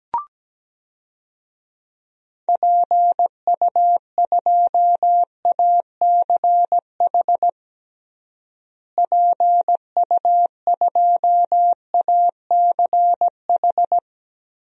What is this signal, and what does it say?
Morse code, heard as a keyed tone, about 17 wpm, 700 Hz. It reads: PU2ACH PU2ACH
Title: DXCC-100 Call Signs - 6 Characters - Repeated 3x - 17wpm